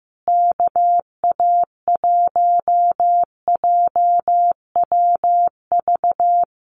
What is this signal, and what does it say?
Morse code, heard as a keyed tone, about 15 wpm, 700 Hz. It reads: KA1JWV
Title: Common QSO Phrases Encore 15wpm